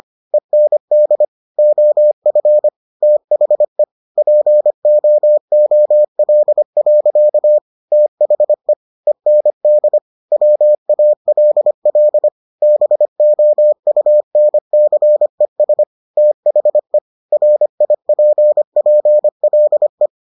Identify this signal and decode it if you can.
Morse code, heard as a keyed tone, about 25 wpm, 600 Hz. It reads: END OF THE POOL. THE END WALL BOUNCES THE RIPPLE